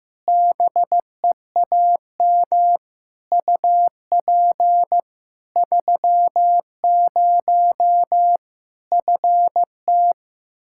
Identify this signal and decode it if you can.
Morse code, heard as a keyed tone, about 15 wpm, 700 Hz. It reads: BEAM UP 30 FT
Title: Practice QSO - Exchange 2 - Info - 15wpm